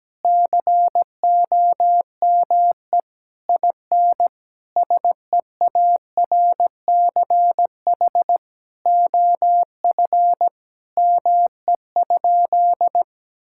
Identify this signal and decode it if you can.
Morse code, heard as a keyed tone, about 17 wpm, 700 Hz. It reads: COME IN SEARCH OF ME?
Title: Sentences from Top 1000 Words - Review Part 5 - 17wpm